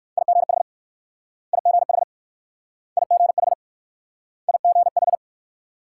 Morse code, heard as a keyed tone, about 56 wpm, 690 Hz. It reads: IGH IGH IGH IGH
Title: Speed-Racing Multi-letter Phonemes - 45wpm